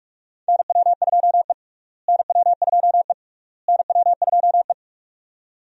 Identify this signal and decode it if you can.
Morse code, heard as a keyed tone, about 46 wpm, 700 Hz. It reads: NW2E NW2E NW2E
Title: USA Call Signs - 4 Characters - Repeated 3x - 45wpm